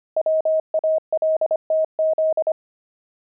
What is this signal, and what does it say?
Morse code, heard as a keyed tone, about 25 wpm, 630 Hz. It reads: WALTZ
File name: pangram-fs-25-cs-25-devils